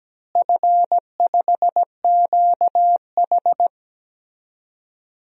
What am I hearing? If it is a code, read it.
Morse code, heard as a keyed tone, about 17 wpm, 700 Hz. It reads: F5QH
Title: DXCC-100 Call Signs - 4 Characters - Repeated 3x - 17wpm